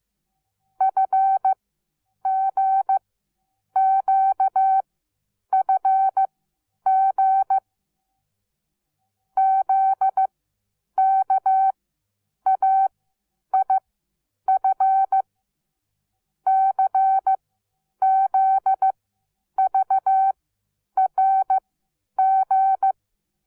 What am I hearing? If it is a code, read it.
Morse code, heard as a keyed tone, about 15 wpm, spaced as for 9 wpm, 770 Hz. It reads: FGQFG ZKAIF CZVRG
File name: Onlinelehrgang / morsen / DJ4UF Morsen17